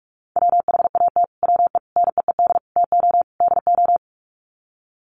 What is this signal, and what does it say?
Morse code, heard as a keyed tone, about 44 wpm, 700 Hz. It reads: WHAT WE NEED TO DO
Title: Sets of 5 Words Encore 45wpm